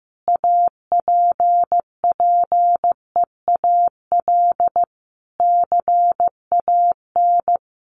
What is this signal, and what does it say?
Morse code, heard as a keyed tone, about 15 wpm, 700 Hz. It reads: APPEAL CAN
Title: Sentences from Top 1300 Words 15wpm